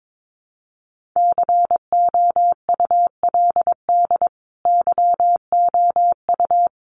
Morse code, heard as a keyed tone, about 22 wpm, 700 Hz. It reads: COULD YOU
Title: Sentences from Top 1400 Words 22wpm